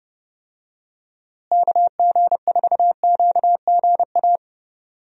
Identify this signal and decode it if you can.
Morse code, heard as a keyed tone, about 30 wpm, 700 Hz. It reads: KG4QGA
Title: USA Call Signs - 6 Characters - Repeated 3x - 30wpm